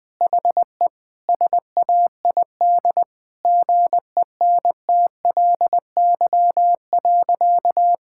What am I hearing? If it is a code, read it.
Morse code, heard as a keyed tone, about 20 wpm, 700 Hz. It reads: HE SAID GENTLY.
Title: Sentences from Top 1600 Words 20wpm